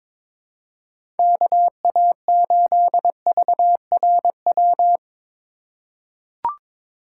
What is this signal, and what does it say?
Morse code, heard as a keyed tone, about 22 wpm, 700 Hz. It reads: KA8VRW
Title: USA Call Signs - 6 Characters - 2x Character Spacing - 22wpm